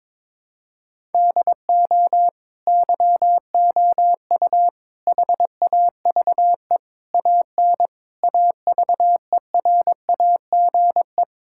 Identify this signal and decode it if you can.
Morse code, heard as a keyed tone, about 22 wpm, 700 Hz. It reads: DO YOU HAVE AN AVERAGE
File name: Sentences with 9-Word Phrases - 22wpm